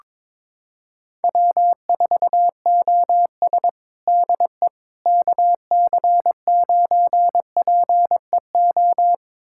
Morse code, heard as a keyed tone, about 22 wpm, 700 Hz. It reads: W4OS DE KC9PEO
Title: USA Callsign DE Callsign 22wpm